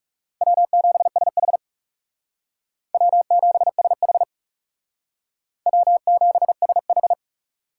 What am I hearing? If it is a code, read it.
Morse code, heard as a keyed tone, about 37 wpm, 700 Hz. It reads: W7SH W7SH W7SH
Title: Speed Racing - USA 1x2 Call Signs 30wpm